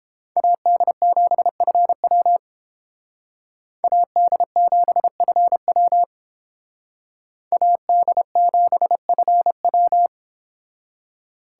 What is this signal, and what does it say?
Morse code, heard as a keyed tone, about 29 wpm, 700 Hz. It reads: AD7FW AD7FW AD7FW